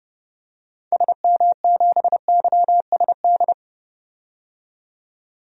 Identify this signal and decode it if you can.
Morse code, heard as a keyed tone, about 30 wpm, 700 Hz. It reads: SM7YSD
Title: DXCC-100 Call Signs - 6 Characters - 30wpm